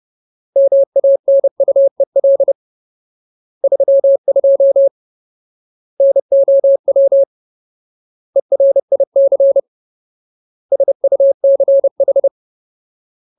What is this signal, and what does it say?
Morse code, heard as a keyed tone, about 30 wpm, 550 Hz. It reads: MANUEL 32 NOW ERIC SUCH
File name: Ultimate Word List / 30WPM Ultimate